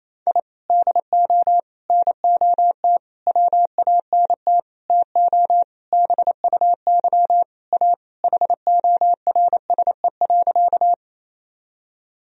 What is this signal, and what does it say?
Morse code, heard as a keyed tone, about 28 wpm, 700 Hz. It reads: I DO NOT WANT TO BUY A HORSE.
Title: Sentences from Top 1000 Words - Review Part 4 - 28wpm